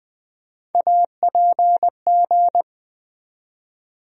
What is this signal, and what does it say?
Morse code, heard as a keyed tone, about 20 wpm, 700 Hz. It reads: APG
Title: USA Call Sign Postfixes - 3 Letters 20wpm